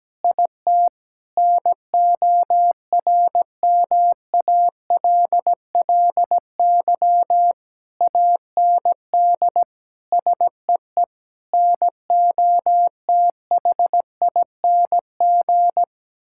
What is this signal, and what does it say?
Morse code, heard as a keyed tone, about 17 wpm, 700 Hz. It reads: IT NORMALLY AND SEE NOTHING
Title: Sentences from Top 1000 Words - Review Part 6 - 17wpm